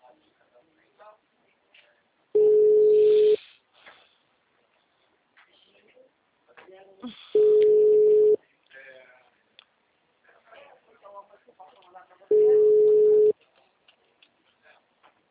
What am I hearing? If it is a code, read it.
Ringback tone (Europe, 425 Hz), with speech.
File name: out- -2023- - - .2822